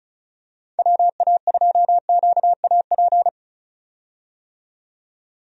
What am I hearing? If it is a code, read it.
Morse code, heard as a keyed tone, about 36 wpm, 700 Hz. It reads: WA2QAP
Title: USA Call Signs - 6 Characters - 35wpm